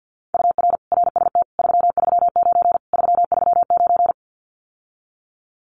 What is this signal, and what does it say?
Morse code, heard as a keyed tone, about 50 wpm, 700 Hz. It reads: UR RST 339 339
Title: Common QSO Phrases Encore 50wpm